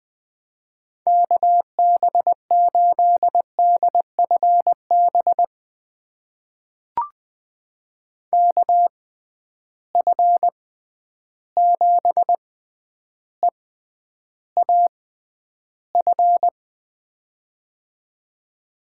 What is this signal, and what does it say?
Morse code, heard as a keyed tone, about 20 wpm, spaced as for 10 wpm, 700 Hz. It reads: KB8DFB KF7EAF
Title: USA Call Signs - 6 Characters - 6x Character Spacing - 20wpm